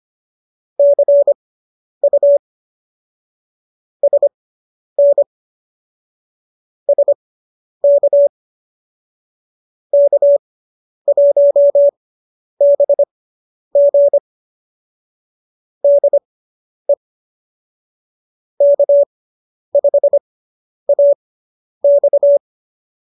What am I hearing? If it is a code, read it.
Morse code, heard as a keyed tone, about 25 wpm, spaced as for 10 wpm, 570 Hz. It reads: CU SN SK K1BG DE K5AX